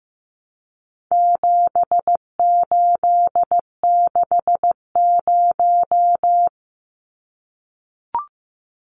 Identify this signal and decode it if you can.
Morse code, heard as a keyed tone, about 15 wpm, 700 Hz. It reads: 7860
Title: Numbers - 4-Digit 15wpm